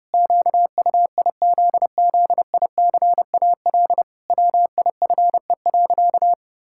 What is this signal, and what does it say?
Morse code, heard as a keyed tone, about 30 wpm, 700 Hz. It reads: QUIZZICAL WIFE.
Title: Pangram Challenge v2 - 30wpm